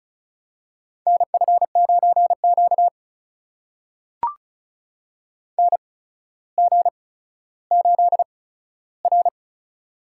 Morse code, heard as a keyed tone, about 36 wpm, spaced as for 15 wpm, 700 Hz. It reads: NF9Q NG8R
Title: USA Call Signs - 4 Characters - 8x Character Spacing - 35wpm